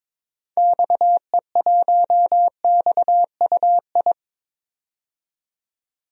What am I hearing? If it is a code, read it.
Morse code, heard as a keyed tone, about 22 wpm, 700 Hz. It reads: XE1XUI
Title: DXCC-100 Call Signs - 6 Characters - 22wpm